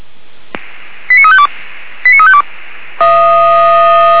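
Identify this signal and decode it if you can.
Five-tone selective call (ZVEI), heard as quick runs of five tones, tones of about 70 ms. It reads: 89352 89352
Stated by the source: ZVEI five-tone selective call 89346 received on SDR.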